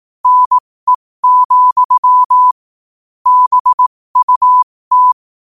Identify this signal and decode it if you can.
Morse code, heard as a keyed tone, about 18 wpm, spaced as for 15 wpm, 1000 Hz. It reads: NE, BUT
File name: qotdfiles / 15wpm